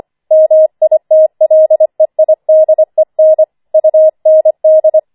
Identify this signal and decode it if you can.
Morse code, heard as a keyed tone, about 25 wpm, 630 Hz. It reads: MITLEIDEN UND
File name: 25wpm